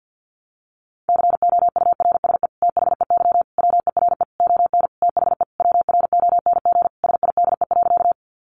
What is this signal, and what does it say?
Morse code, heard as a keyed tone, about 50 wpm, 700 Hz. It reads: COURSE THEY WERE ON THE WRONG SIDE.
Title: Sentences from Top 900 Words 50wpm